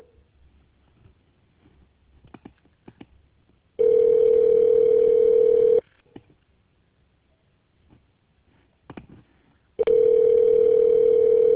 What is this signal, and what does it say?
Ringback tone (North America), 440 + 480 Hz.